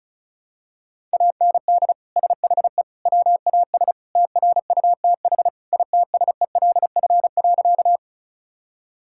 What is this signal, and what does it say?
Morse code, heard as a keyed tone, about 36 wpm, 700 Hz. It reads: AND SHE WAS TRUTH ITSELF.
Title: Sentences from Top 1000 Words - Review Part 5 - 35wpm